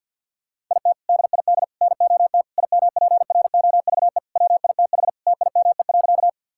Morse code, heard as a keyed tone, about 50 wpm, 700 Hz. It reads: IT DID NOT IMPROVE WITH TIME.
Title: Sentences from Top 1000 Words - Review Part 5 - 50wpm